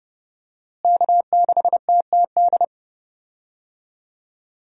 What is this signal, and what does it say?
Morse code, heard as a keyed tone, about 32 wpm, 700 Hz. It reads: K6TTD
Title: USA Call Signs - 5 Characters - Repeated 2x - 30wpm